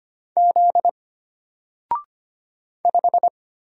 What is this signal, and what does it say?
Morse code, heard as a keyed tone, about 25 wpm, 700 Hz. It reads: Z 5
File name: Single-Letter-Number-25wpm